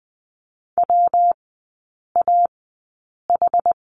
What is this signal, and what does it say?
Morse code, heard as a keyed tone, about 20 wpm, spaced as for 8 wpm, 700 Hz. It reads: WAH
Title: 198 Lesson 32e - Letter k - Three Characters 20wpm